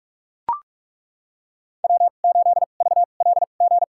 Morse code, heard as a keyed tone, about 43 wpm, spaced as for 32 wpm, 700 Hz. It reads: W8VLG
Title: USA Call Signs - 5 Characters - 2x Character Spacing - 45wpm